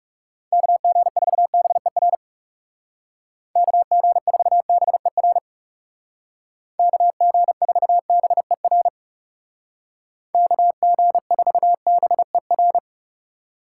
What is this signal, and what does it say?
Morse code, heard as a keyed tone, about 36 wpm, 700 Hz. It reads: KG4BER KG4BER KG4BER KG4BER
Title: Speed Racing - USA Call Signs 30wpm